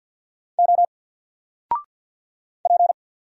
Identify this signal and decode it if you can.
Morse code, heard as a keyed tone, about 50 wpm, 700 Hz. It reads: O P
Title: Single Letter-Number 50wpm